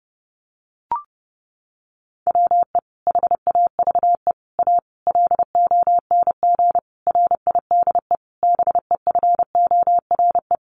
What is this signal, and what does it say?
Morse code, heard as a keyed tone, about 30 wpm, 700 Hz. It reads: WE HAVE A LONG RIDE BEFORE
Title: Sentences from Top 1200 Words 30wpm